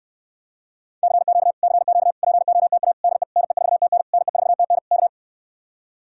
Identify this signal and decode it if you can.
Morse code, heard as a keyed tone, about 69 wpm, 700 Hz. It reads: CQ CQ POTA DE NE4TN NE4TN K